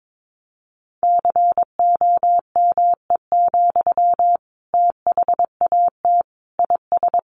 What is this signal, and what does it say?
Morse code, heard as a keyed tone, about 22 wpm, 700 Hz. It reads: COME, THAT IS
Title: Sentences from Top 700 Words 22wpm